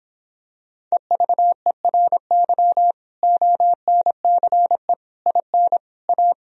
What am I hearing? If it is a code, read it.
Morse code, heard as a keyed tone, about 26 wpm, 700 Hz. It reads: EVERY ONCE IN A